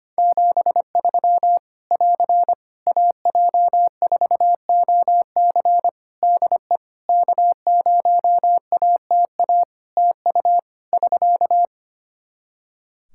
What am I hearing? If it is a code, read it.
Morse code, heard as a keyed tone, about 25 wpm, 700 Hz. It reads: 73 <AR> AJ4OC DE K0ATA TU <SK>
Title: Practice QSO - Exchange 4 - Ending - 25wpm